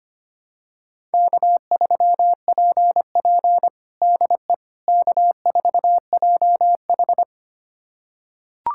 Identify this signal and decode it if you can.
Morse code, heard as a keyed tone, about 25 wpm, 700 Hz. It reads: K3PP DE K4JH
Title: USA Callsign DE Callsign - 4 Characters - 25wpm